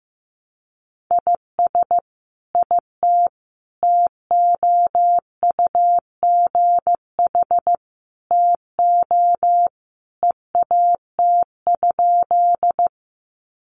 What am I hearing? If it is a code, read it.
Morse code, heard as a keyed tone, about 15 wpm, 700 Hz. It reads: IS IT TOUGH TO EAT?
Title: Sentences from Top 1200 Words 15wpm